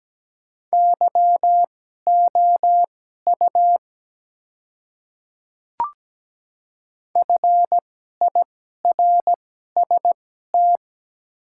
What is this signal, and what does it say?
Morse code, heard as a keyed tone, about 17 wpm, spaced as for 12 wpm, 700 Hz. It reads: YOU FIRST
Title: 2x Character Spacing - Top 100 Words - 17wpm